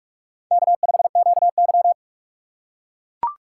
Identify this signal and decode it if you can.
Morse code, heard as a keyed tone, about 42 wpm, 700 Hz. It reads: K5QY
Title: USA Call Signs - 4 Characters - 8x Character Spacing - 45wpm